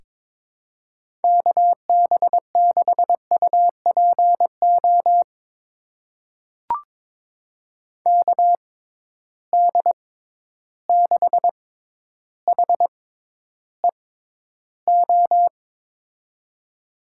Morse code, heard as a keyed tone, about 22 wpm, spaced as for 11 wpm, 700 Hz. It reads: KB6UPO KD6HEO
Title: USA Call Signs - 6 Characters - 6x Character Spacing - 22wpm